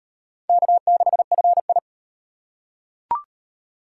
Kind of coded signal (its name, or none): Morse code